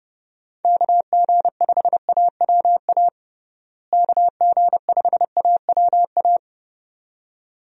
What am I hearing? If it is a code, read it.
Morse code, heard as a keyed tone, about 30 wpm, 700 Hz. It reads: KG5AWA KG5AWA
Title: USA Call Signs - 6 Characters - Repeated 2x - 30wpm